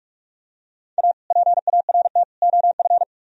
Morse code, heard as a keyed tone, about 49 wpm, 700 Hz. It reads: A PART OF